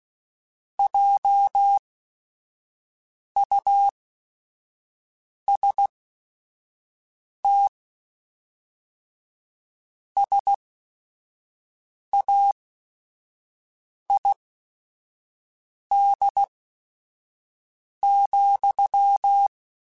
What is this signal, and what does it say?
Morse code, heard as a keyed tone, about 16 wpm, spaced as for 5 wpm, 800 Hz. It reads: JUST SAID,